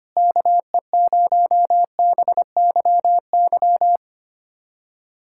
Morse code, heard as a keyed tone, about 25 wpm, 700 Hz. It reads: KE0BYY
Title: USA Callsign DE Callsign - 6 Characters - 25wpm